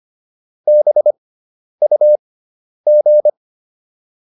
Morse code, heard as a keyed tone, about 25 wpm, spaced as for 10 wpm, 600 Hz. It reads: BUG